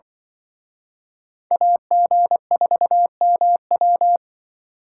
Morse code, heard as a keyed tone, about 24 wpm, 700 Hz. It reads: AG4MW
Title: Speed Racing - USA 2x2 Call Signs 20wpm